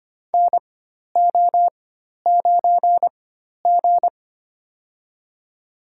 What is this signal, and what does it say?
Morse code, heard as a keyed tone, about 25 wpm, spaced as for 12 wpm, 700 Hz. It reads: NO9G
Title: USA Call Signs - 4 Characters - 4x Character Spacing - 25wpm